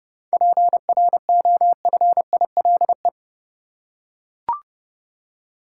Morse code, heard as a keyed tone, about 30 wpm, 700 Hz. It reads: PROFILE
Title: Seven Letter Words - 30wpm